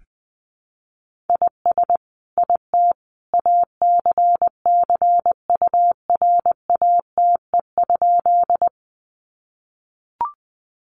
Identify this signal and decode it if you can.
Morse code, heard as a keyed tone, about 20 wpm, 700 Hz. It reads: IS IT ACCURATE?
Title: Sentences from Top 1200 Words 20wpm